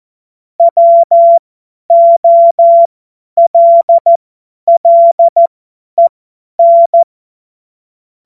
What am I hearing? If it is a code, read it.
Morse code, heard as a keyed tone, about 14 wpm, spaced as for 10 wpm, 680 Hz. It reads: WOLLEN
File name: Witz-19-August-2025-50-F70